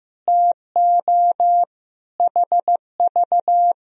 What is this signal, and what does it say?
Morse code, heard as a keyed tone, about 16 wpm, 700 Hz. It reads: TO HV
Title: Sentences from Top 100 Words and Abbreviations 15wpm